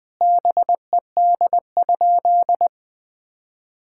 Morse code, heard as a keyed tone, about 21 wpm, 700 Hz. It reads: BED?
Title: Sentences from Top 1000 Words - Review Part 6 - 20wpm